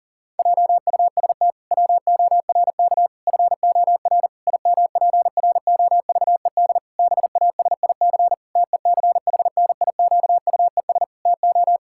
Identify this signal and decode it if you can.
Morse code, heard as a keyed tone, about 40 wpm, 700 Hz. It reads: JUST WORK FOR IMPROVED BASIC TECHNIQUES TO
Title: Pangram Challenge v2 - 40wpm